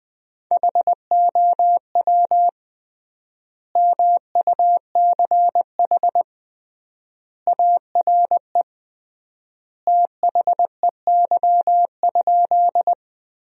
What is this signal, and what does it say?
Morse code, heard as a keyed tone, about 20 wpm, 700 Hz. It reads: HOW MUCH ARE THEY?
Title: 3x Word-Spacing - Sentences from Top 200 Words - 20wpm